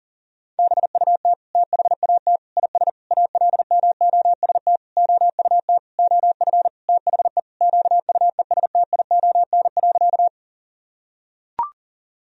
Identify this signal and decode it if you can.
Morse code, heard as a keyed tone, about 40 wpm, 700 Hz. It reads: BUT THAT IS ALMOST OUT OF THE QUESTION.
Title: Sentences from Top 1000 Words - Review Part 4 - 40wpm